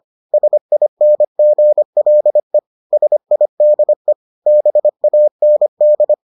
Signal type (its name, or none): Morse code